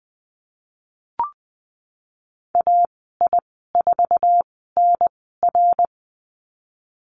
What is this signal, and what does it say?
Morse code, heard as a keyed tone, about 20 wpm, spaced as for 15 wpm, 700 Hz. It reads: AI4NR
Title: 055 Lesson 10g - Letter r - Call Signs 20wpm